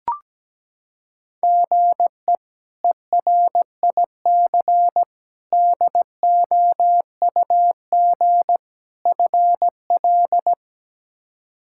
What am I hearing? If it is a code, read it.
Morse code, heard as a keyed tone, about 17 wpm, 700 Hz. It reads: GE ERIC DOUG FL
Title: K1USN SST QSO Practice - 17wpm